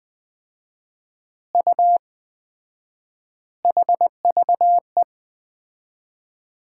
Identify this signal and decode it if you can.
Morse code, heard as a keyed tone, about 20 wpm, 700 Hz. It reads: U HVE
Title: Common QSO Phrases - 4x Word-Spacing - 20wpm